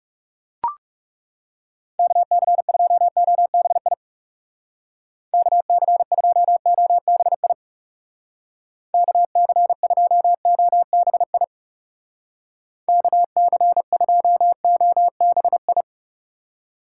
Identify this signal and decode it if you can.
Morse code, heard as a keyed tone, about 37 wpm, 700 Hz. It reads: KC2OBI KC2OBI KC2OBI KC2OBI
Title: Speed Racing - USA 2x3 Call Signs 30wpm